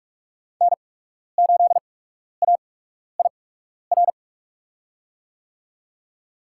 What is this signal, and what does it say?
Morse code, heard as a keyed tone, about 45 wpm, spaced as for 12 wpm, 690 Hz. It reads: N8AIR
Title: USA Call Signs - 5 Characters - 8x Character Spacing - 45wpm